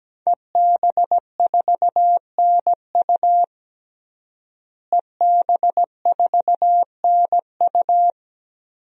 Morse code, heard as a keyed tone, about 17 wpm, 700 Hz. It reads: EB4NU EB4NU
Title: DXCC-100 Call Signs - 5 Characters - Repeated 2x - 17wpm